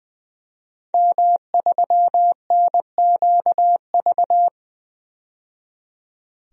Morse code, heard as a keyed tone, about 20 wpm, 700 Hz. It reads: M3NQV
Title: DXCC-100 Call Signs - 5 Characters - 20wpm